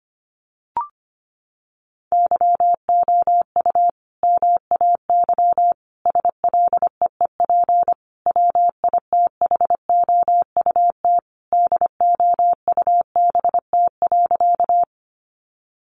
Morse code, heard as a keyed tone, about 25 wpm, 700 Hz. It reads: YOU MAY SLEEP WITHOUT DOUBT.
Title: Sentences from Top 1500 Words 25wpm